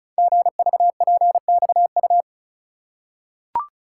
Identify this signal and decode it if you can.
Morse code, heard as a keyed tone, about 36 wpm, 700 Hz. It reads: GVPXU
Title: Code Groups - 5 Letters - 35wpm